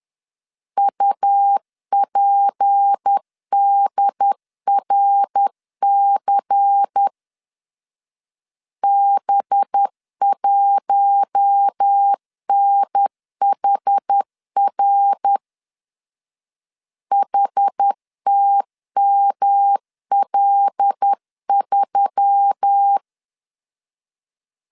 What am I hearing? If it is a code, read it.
Morse code, heard as a keyed tone, about 10 wpm, 800 Hz. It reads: UPDRC B1NHR HTML3
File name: LEZIONI / 6 PKX0 / 6 P 10wpm 2432